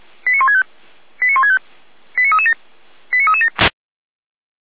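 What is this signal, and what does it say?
Five-tone selective call (ZVEI), heard as quick runs of five tones, tones of about 80 ms. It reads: 89165 89165 89337 89337
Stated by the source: Five-tone ZVEI selective call, received by SDR.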